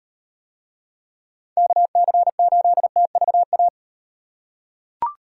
Morse code, heard as a keyed tone, about 38 wpm, 700 Hz. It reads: KC8TVA